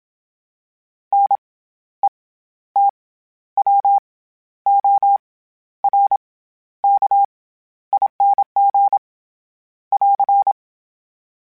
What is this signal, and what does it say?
Morse code, heard as a keyed tone, about 27 wpm, 800 Hz. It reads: N E T W O R K ING <AR>
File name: Special INGs-mp3s / Special INGs-mp3s / SlowToFaster ing @25